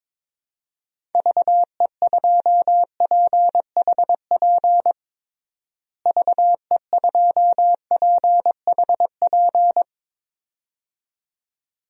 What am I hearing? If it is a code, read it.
Morse code, heard as a keyed tone, about 22 wpm, 700 Hz. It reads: VE2PHP VE2PHP
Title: DXCC-100 Call Signs - 6 Characters - Repeated 2x - 22wpm